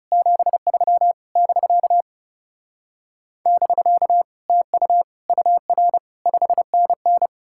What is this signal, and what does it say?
Morse code, heard as a keyed tone, about 37 wpm, 700 Hz. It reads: 73 <BK> <BK> TU UR 5NN
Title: Speed-Racing Common QSO Phrases - 30wpm